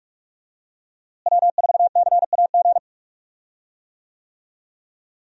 Morse code, heard as a keyed tone, about 44 wpm, 700 Hz. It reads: W4CAG
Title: USA Call Signs - 5 Characters - Repeated 2x - 45wpm